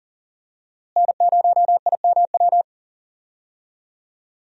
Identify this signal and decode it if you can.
Morse code, heard as a keyed tone, about 40 wpm, 700 Hz. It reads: N0IMW
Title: USA Call Signs - 5 Characters - Repeated 2x - 40wpm